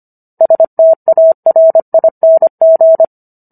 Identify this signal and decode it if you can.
Morse code, heard as a keyed tone, about 25 wpm, 650 Hz. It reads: STARING